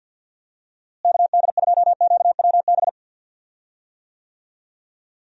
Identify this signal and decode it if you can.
Morse code, heard as a keyed tone, about 51 wpm, 700 Hz. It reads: KD2QWB
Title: USA Call Signs - 6 Characters - Repeated 3x - 50wpm